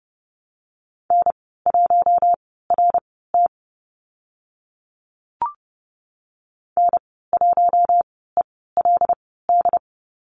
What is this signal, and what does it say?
Morse code, heard as a keyed tone, about 30 wpm, spaced as for 17 wpm, 700 Hz. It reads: N1RT N1ELD
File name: Lesson 15g - Number 5 - Call Signs 30wpm